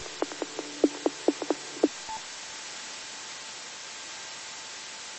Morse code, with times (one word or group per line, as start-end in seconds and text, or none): none